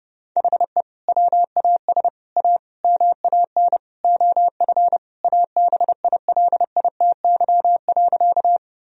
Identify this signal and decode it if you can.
Morse code, heard as a keyed tone, about 30 wpm, 700 Hz. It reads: HE WAS A MAN OF ABILITY.
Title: Sentences from Top 1000 Words - Review Part 3 - 30wpm